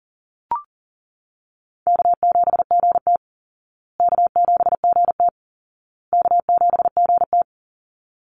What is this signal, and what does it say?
Morse code, heard as a keyed tone, about 40 wpm, 700 Hz. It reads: K7GT K7GT K7GT
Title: Easy Call Signs Repeated 3x - 40wpm